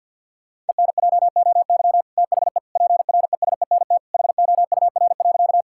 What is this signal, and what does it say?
Morse code, heard as a keyed tone, about 50 wpm, 700 Hz. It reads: ENJOY THE PRESENT HOUR.
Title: Sentences from Top 1000 Words - Review Part 5 - 50wpm